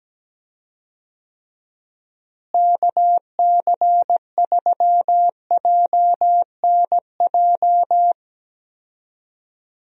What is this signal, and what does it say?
Morse code, heard as a keyed tone, about 17 wpm, 700 Hz. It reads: KC3JNJ
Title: USA Call Signs - 6 Characters - Repeated 2x - 17wpm